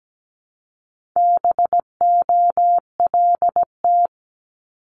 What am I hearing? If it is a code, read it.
Morse code, heard as a keyed tone, about 17 wpm, 700 Hz. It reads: BOLT
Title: Multi-letter Phonemes Repeated 3x 17wpm